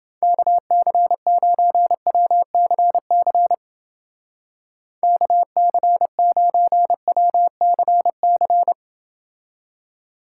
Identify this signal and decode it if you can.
Morse code, heard as a keyed tone, about 29 wpm, 700 Hz. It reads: KC9WCC KC9WCC